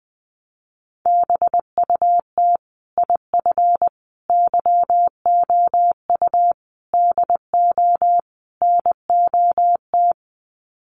Morse code, heard as a keyed tone, about 20 wpm, 700 Hz. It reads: BUT IF YOU DO NOT
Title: Sets of 5 Words Encore 20wpm